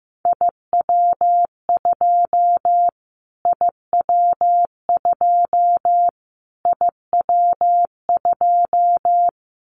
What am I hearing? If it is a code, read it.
Morse code, heard as a keyed tone, about 15 wpm, 700 Hz. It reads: IW2 IW2 IW2
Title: Callsign Prefixes for Top 25 DXCC - Repeated 3x - 15wpm